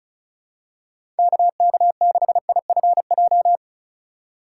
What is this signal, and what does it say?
Morse code, heard as a keyed tone, about 36 wpm, 700 Hz. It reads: KK6IFJ
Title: USA Call Signs - 6 Characters - 6x Character Spacing - 35wpm